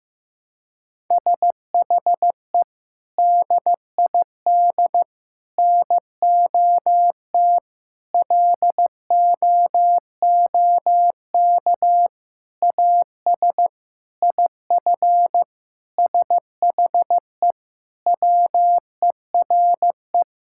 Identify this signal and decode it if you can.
Morse code, heard as a keyed tone, about 15 wpm, 700 Hz. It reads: SHE DID NOT LOOK AS IF SHE WERE
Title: Sentences from Top 1000 Words - Review Part 6 - 15wpm